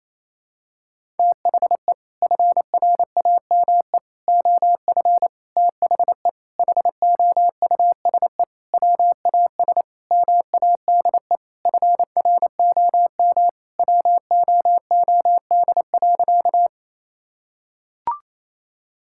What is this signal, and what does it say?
Morse code, heard as a keyed tone, about 27 wpm, 700 Hz. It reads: THE FRAME OF THE HOUSE WAS MADE FROM WOOD.
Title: Sentences from Top 1000 Words - Review Part 6 - 28wpm